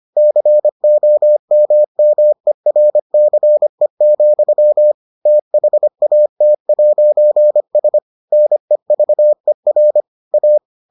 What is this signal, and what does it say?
Morse code, heard as a keyed tone, about 25 wpm, 600 Hz. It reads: COMMERCE, THAT'S NEVER A